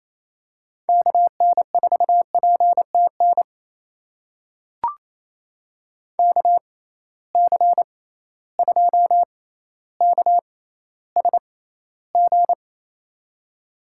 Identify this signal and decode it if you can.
Morse code, heard as a keyed tone, about 28 wpm, spaced as for 14 wpm, 700 Hz. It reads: KN4PTN KC2KSG
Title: USA Call Signs - 6 Characters - 6x Character Spacing - 28wpm